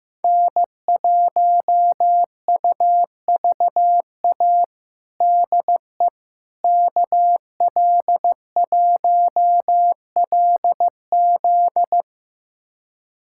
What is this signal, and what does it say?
Morse code, heard as a keyed tone, about 16 wpm, 700 Hz. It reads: N1UVA DE KL1LZ
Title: USA Callsign DE Callsign - 5 Characters - 15wpm